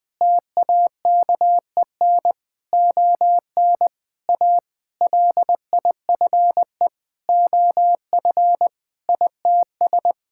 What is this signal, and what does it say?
Morse code, heard as a keyed tone, about 20 wpm, 700 Hz. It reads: TAKEN ON A LIFE OF ITS
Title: Sentences with 7-Word Phrases - Spoken Beginning - 20wpm